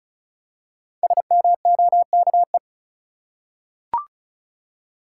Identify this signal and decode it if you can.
Morse code, heard as a keyed tone, about 34 wpm, 700 Hz. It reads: SMOKE
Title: Five Letter Words - 35wpm